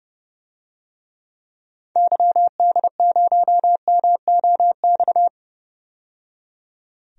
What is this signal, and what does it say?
Morse code, heard as a keyed tone, about 30 wpm, 700 Hz. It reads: YD0MOX
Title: DXCC-100 Call Signs - 6 Characters - 30wpm